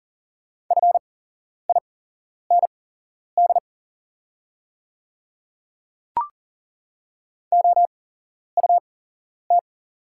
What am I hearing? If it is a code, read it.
Morse code, heard as a keyed tone, about 40 wpm, spaced as for 11 wpm, 700 Hz. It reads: FIND OUT
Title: 8x Character Spacing - Top 100 Words - 40wpm